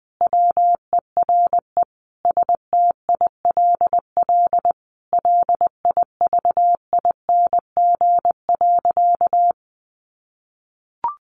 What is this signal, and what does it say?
Morse code, heard as a keyed tone, about 20 wpm, 700 Hz. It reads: WERE STILL LIVING.
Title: Sentences from Top 1100 Words 20wpm